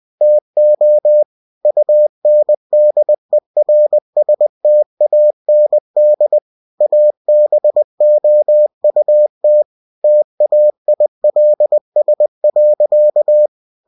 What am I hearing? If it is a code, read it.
Morse code, heard as a keyed tone, about 20 wpm, 600 Hz. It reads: TO UNDERSTAND ABOUT TAILS.
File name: cw-books / winnie-the-pooh / winnie-20wpm-page-0088